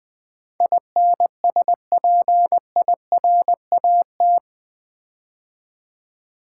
Morse code, heard as a keyed tone, about 21 wpm, 700 Hz. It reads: INSPIRAT
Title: Build A Word - 4 Letters At A Time - 11-Letter Words - 20wpm